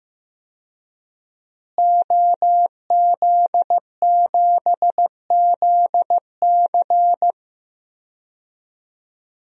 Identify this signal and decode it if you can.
Morse code, heard as a keyed tone, about 16 wpm, 700 Hz. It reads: OZ7ZC
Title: DXCC-100 Call Signs - 5 Characters - Repeated 2x - 15wpm